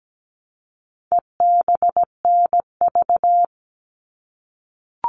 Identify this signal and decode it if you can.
Morse code, heard as a keyed tone, about 17 wpm, 700 Hz. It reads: EBNV
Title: All Cut Numbers - 4-Digit 17wpm